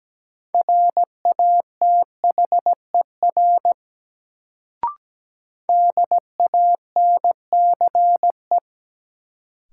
Morse code, heard as a keyed tone, about 17 wpm, 700 Hz. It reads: RATHER DANCE
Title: Sentences from Top 700 Words - New Word Drill - 17wpm